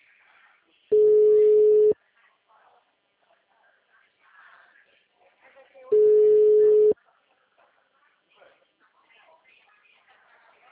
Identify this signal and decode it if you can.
Ringback tone (Europe, 425 Hz), with speech.
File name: out- -2005- - - .58